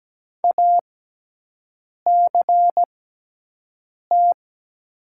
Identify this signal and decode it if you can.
Morse code, heard as a keyed tone, about 17 wpm, spaced as for 6 wpm, 700 Hz. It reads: ACT
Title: 6x Character Spacing - Top 200 Words - 17wpm